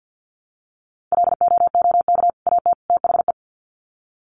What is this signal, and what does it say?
Morse code, heard as a keyed tone, about 49 wpm, 700 Hz. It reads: LOOK AT THE
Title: Sets of 3 Words 50wpm